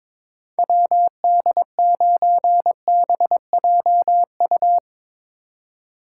Morse code, heard as a keyed tone, about 21 wpm, 700 Hz. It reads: WD9BJU
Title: USA Call Signs - 6 Characters - 3x Character Spacing - 22wpm